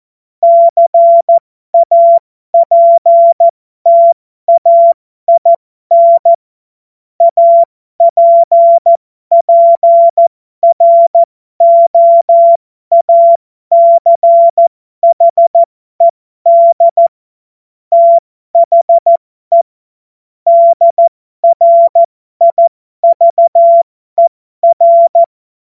Morse code, heard as a keyed tone, about 14 wpm, spaced as for 12 wpm, 680 Hz. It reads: CAPTAIN APPROACHED THE DRIVER
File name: Morsekurs / Witze